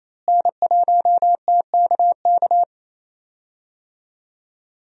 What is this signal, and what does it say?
Morse code, heard as a keyed tone, about 28 wpm, 700 Hz. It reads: N1TKK
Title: USA Call Signs - 5 Characters - Repeated 3x - 28wpm